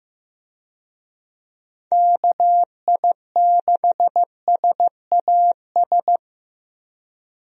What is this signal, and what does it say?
Morse code, heard as a keyed tone, about 15 wpm, 700 Hz. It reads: KI6SAS
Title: USA Call Signs - 6 Characters - 15wpm